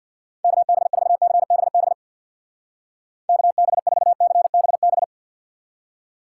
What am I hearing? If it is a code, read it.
Morse code, heard as a keyed tone, about 63 wpm, 700 Hz. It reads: KB4CBB KB4CBB